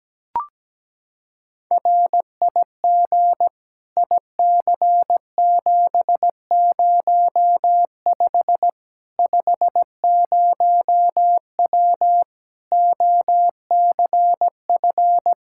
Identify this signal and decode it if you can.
Morse code, heard as a keyed tone, about 17 wpm, 700 Hz. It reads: RIG IC705 50W OCF
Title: Practice QSO - Exchange 2 - Info - 17wpm